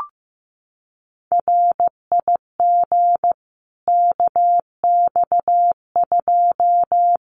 Morse code, heard as a keyed tone, about 15 wpm, 700 Hz. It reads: RIG KX2
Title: Common QSO Phrases 15wpm